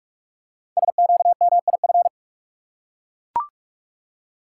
Morse code, heard as a keyed tone, about 44 wpm, 700 Hz. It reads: SQMIF
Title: Code Groups - 5 Letters - 45wpm